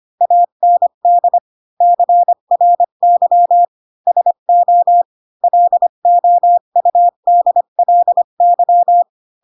Morse code, heard as a keyed tone, about 25 wpm, 700 Hz. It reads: AND CRY SO LOUDLY
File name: Pinocchio / Pinocchio ch7 20 25 ik1hge